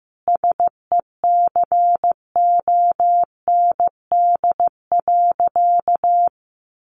Morse code, heard as a keyed tone, about 15 wpm, 700 Hz. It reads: SECOND.
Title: A Sound Of Thunder 15wpm